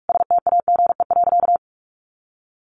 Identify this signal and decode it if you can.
Morse code, heard as a keyed tone, about 44 wpm, 700 Hz. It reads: STAGE.
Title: Sentences from Top 1200 Words 45wpm